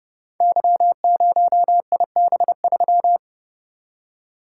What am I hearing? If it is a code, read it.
Morse code, heard as a keyed tone, about 29 wpm, 700 Hz. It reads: Y0IB3
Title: Code Groups - 5 Letter:Numbers - 30wpm